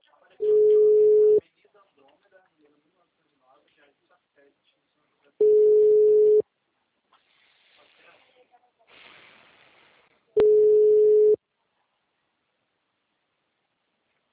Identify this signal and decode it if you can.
Ringback tone (Europe, 425 Hz), with speech.
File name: out- -2002- - - .40989